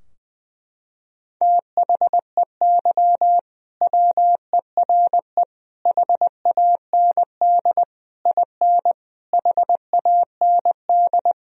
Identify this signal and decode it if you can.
Morse code, heard as a keyed tone, about 19 wpm, 700 Hz. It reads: THEY WERE HAND IN HAND
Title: Sentences from Top 1000 Words - Review Part 2 - 20wpm